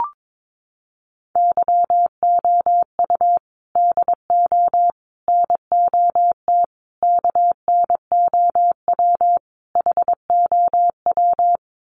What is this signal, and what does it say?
Morse code, heard as a keyed tone, about 22 wpm, 700 Hz. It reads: YOU DO NOT KNOW HOW
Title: Sets of 5 Words Encore 22wpm